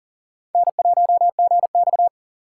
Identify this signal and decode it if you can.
Morse code, heard as a keyed tone, about 39 wpm, 700 Hz. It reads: N1GX
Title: USA Call Signs - 4 Characters - 3x Character Spacing - 40wpm